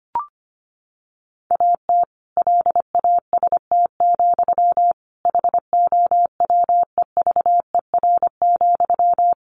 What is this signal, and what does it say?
Morse code, heard as a keyed tone, about 25 wpm, 700 Hz. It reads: AT LAST, HOWEVER,
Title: Sentences from Top 1400 Words 25wpm